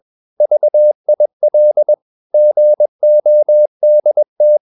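Morse code, heard as a keyed tone, about 21 wpm, 600 Hz. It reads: VIL GODT
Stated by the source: Morse practice at 21 wpm, slow-speed holiday exercise.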